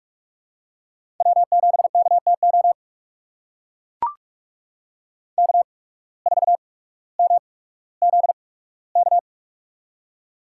Morse code, heard as a keyed tone, about 44 wpm, spaced as for 19 wpm, 690 Hz. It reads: W7KTO K4MZK
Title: USA Call Signs - 5 Characters - 8x Character Spacing - 45wpm